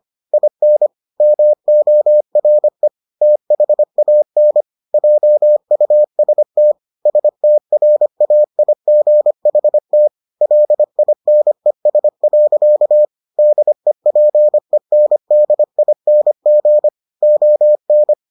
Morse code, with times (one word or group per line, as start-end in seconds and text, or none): 0.3-0.9 IN
1.2-2.9 MORE
3.2-4.6 THAN
4.9-6.7 JUST
7.1-10.1 STRAIGHT
10.4-13.1 LINES.
13.4-16.9 DEPENDING
17.2-18.1 ON